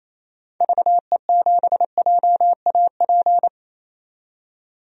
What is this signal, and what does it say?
Morse code, heard as a keyed tone, about 28 wpm, 700 Hz. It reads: VE7JAP
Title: DXCC-100 Call Signs - 6 Characters - 28wpm